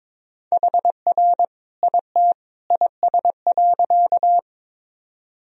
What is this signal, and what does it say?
Morse code, heard as a keyed tone, about 21 wpm, 700 Hz. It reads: HR IT IS.
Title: Sentences from Top 200 Words and Abbreviations 22wpm